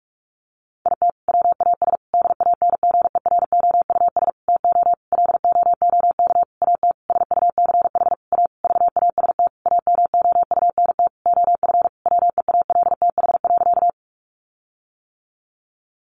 Morse code, heard as a keyed tone, about 44 wpm, 700 Hz. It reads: IT WAS DANGEROUS TO LOOK AT SUCH A VAST AMOUNT OF WEALTH.
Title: Sentences from Top 1300 Words 45wpm